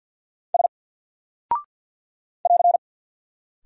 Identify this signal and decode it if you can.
Morse code, heard as a keyed tone, about 48 wpm, 700 Hz. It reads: S <AR>
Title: Single Character (Letter:Number:Punctuation:Prosign) 50wpm